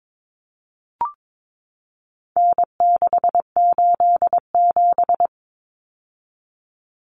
Morse code, heard as a keyed tone, about 22 wpm, 700 Hz. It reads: N687
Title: Common Cut Numbers - 4-Digit 22wpm